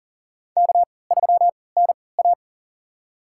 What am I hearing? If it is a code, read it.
Morse code, heard as a keyed tone, about 40 wpm, spaced as for 23 wpm, 700 Hz. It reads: K3NA
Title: USA Call Signs - 4 Characters - 3x Character Spacing - 40wpm